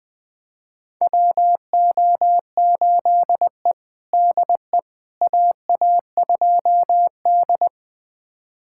Morse code, heard as a keyed tone, about 20 wpm, 700 Hz. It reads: WO8E DE AA2D
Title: USA Callsign DE Callsign - 4 Characters - 20wpm